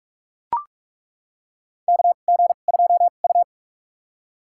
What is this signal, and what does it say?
Morse code, heard as a keyed tone, about 42 wpm, spaced as for 31 wpm, 700 Hz. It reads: KG2U
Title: USA Call Signs - 4 Characters - 2x Character Spacing - 45wpm